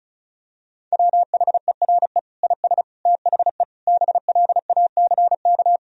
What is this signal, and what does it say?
Morse code, heard as a keyed tone, about 36 wpm, 700 Hz. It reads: WHERE IS THE BLACK
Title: Sentences from Top 1000 Words - Review Part 2 - 35wpm